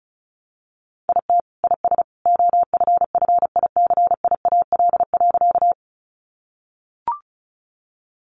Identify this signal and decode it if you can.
Morse code, heard as a keyed tone, about 34 wpm, 700 Hz. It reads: IT IS OFFICIAL.
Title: Sentences from Top 1000 Words 35wpm